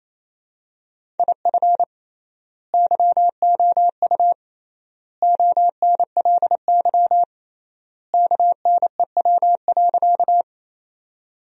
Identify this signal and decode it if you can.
Morse code, heard as a keyed tone, about 28 wpm, 700 Hz. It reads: IF YOU ONLY KNEW.
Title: 3x Word-Spacing - Sentences from Top 400 Words - 28wpm